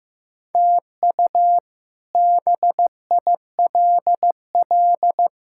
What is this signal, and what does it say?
Morse code, heard as a keyed tone, about 16 wpm, 700 Hz. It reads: TU BILL
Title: K1USN SST QSO Practice - 15wpm